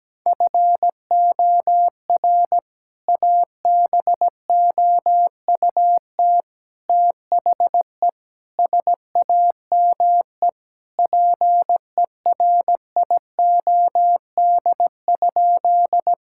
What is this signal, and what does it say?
Morse code, heard as a keyed tone, about 17 wpm, 700 Hz. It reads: FOR ABOUT THE SAME PERIOD?
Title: Sentences from Top 1000 Words - Review Part 5 - 17wpm